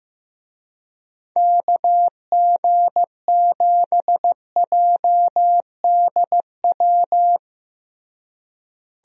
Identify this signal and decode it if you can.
Morse code, heard as a keyed tone, about 15 wpm, 700 Hz. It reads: KG7JDW
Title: USA Call Signs - 6 Characters - Repeated 2x - 15wpm